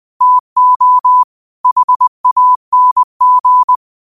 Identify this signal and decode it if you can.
Morse code, heard as a keyed tone, about 20 wpm, 1000 Hz. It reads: TO HANG
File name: qotdfiles / 20wpm